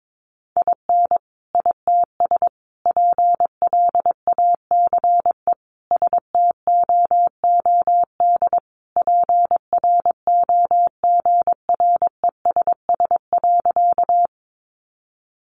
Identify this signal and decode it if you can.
Morse code, heard as a keyed tone, about 22 wpm, 700 Hz. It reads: IN ITS PLACE STOOD PROGRESS.
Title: Sentences from Top 1400 Words 22wpm